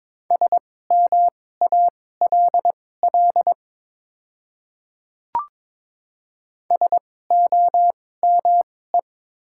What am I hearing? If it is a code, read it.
Morse code, heard as a keyed tone, about 21 wpm, spaced as for 16 wpm, 700 Hz. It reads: SMALL SOME
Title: 2x Character Spacing - Top 300 Words - 22wpm